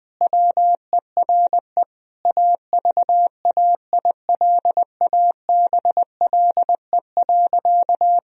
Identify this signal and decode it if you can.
Morse code, heard as a keyed tone, about 20 wpm, 700 Hz. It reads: WERE AVAILABLE.
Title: Sentences from Top 1000 Words - Review Part 3 - 20wpm